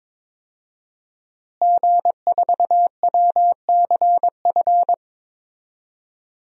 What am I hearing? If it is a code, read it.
Morse code, heard as a keyed tone, about 22 wpm, 700 Hz. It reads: G4WCF
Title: DXCC-100 Call Signs - 5 Characters - Repeated 3x - 22wpm